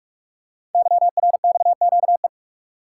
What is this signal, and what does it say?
Morse code, heard as a keyed tone, about 44 wpm, 700 Hz. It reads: YRXQE
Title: Code Groups - 5 Letters - 45wpm